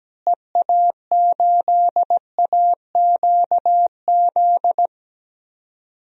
Morse code, heard as a keyed tone, about 17 wpm, 700 Hz. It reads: EA8AQZ
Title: DXCC-100 Call Signs - 6 Characters - Repeated 2x - 17wpm